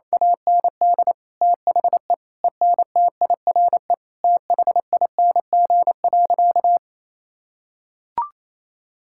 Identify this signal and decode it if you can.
Morse code, heard as a keyed tone, about 28 wpm, 700 Hz. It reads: AND THE ENTIRE THING.
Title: Sentences from Top 1000 Words - Review Part 5 - 28wpm